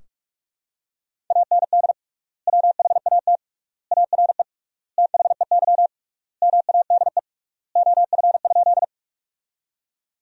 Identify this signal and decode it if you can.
Morse code, heard as a keyed tone, about 41 wpm, 700 Hz. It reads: AND WHAT ARE THEY MADE OF?
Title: 3x Word-Spacing - Sentences from Top 200 Words - 45wpm